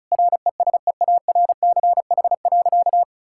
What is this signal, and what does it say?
Morse code, heard as a keyed tone, about 36 wpm, 700 Hz. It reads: RESEARCH.
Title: Sentences from Top 1000 Words - Review Part 5 - 35wpm